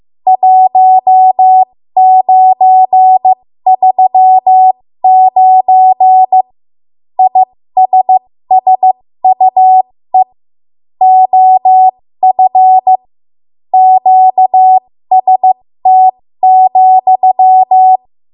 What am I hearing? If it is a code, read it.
Morse code, heard as a keyed tone, about 15 wpm, spaced as for 13 wpm, 750 Hz. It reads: 1939 ISSUE OF QST,
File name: Morse / 13WPM